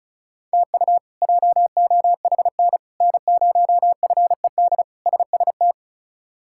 Morse code, heard as a keyed tone, about 34 wpm, 700 Hz. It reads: TU JOHN N0FED SST
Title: K1USN SST QSO Practice - 35wpm